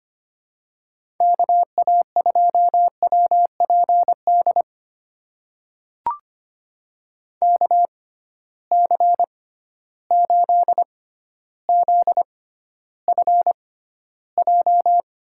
Morse code, heard as a keyed tone, about 25 wpm, spaced as for 13 wpm, 700 Hz. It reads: KA2WPD KC8ZFJ
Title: USA Call Signs - 6 Characters - 6x Character Spacing - 25wpm